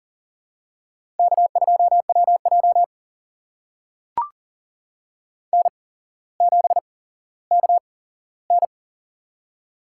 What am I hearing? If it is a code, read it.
Morse code, heard as a keyed tone, about 40 wpm, spaced as for 17 wpm, 700 Hz. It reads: K2WJ N7KN
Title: USA Call Signs - 4 Characters - 8x Character Spacing - 40wpm